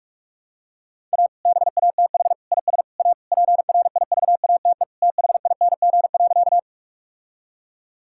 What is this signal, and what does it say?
Morse code, heard as a keyed tone, about 46 wpm, 700 Hz. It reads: A BATH IS A PRIVATE THING.
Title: Sentences from Top 1600 Words 45wpm